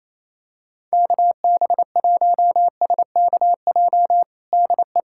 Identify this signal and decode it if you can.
Morse code, heard as a keyed tone, about 27 wpm, 700 Hz. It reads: KB1SKJ DE
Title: USA Callsign DE Callsign 28wpm